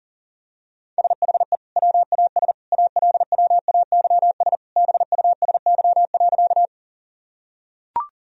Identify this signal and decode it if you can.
Morse code, heard as a keyed tone, about 37 wpm, 700 Hz. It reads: SHE WAS ALWAYS BUSY.
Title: Sentences from Top 1000 Words - Review Part 3 - 40wpm